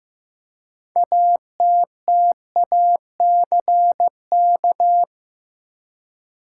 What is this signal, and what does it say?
Morse code, heard as a keyed tone, about 15 wpm, 700 Hz. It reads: ATTACK
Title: Six Letter Words - 15wpm